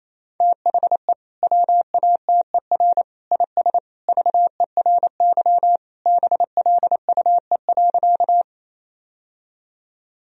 Morse code, heard as a keyed tone, about 28 wpm, 700 Hz. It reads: THE WATER IS VERY BLUE.
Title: Sentences from Top 1000 Words - Review Part 3 - 28wpm